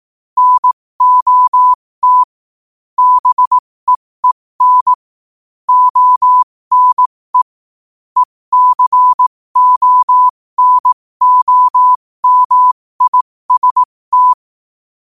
Morse code, heard as a keyed tone, about 18 wpm, spaced as for 15 wpm, 1000 Hz. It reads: NOT BEEN ONE ECONOMIST
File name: qotdfiles / 15wpm